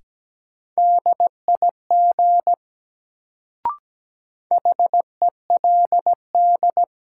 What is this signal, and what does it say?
Morse code, heard as a keyed tone, about 17 wpm, 700 Hz. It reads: DIG HELD
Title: Sentences from Top 1600 Words - New Word Drill - 17wpm